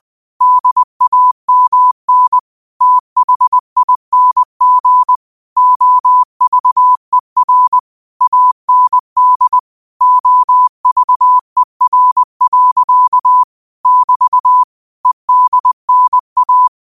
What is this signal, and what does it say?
Morse code, heard as a keyed tone, about 20 wpm, 1000 Hz. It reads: DAMN THING OVER AND OVER. <BT> EDNA
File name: qotdfiles / 20wpm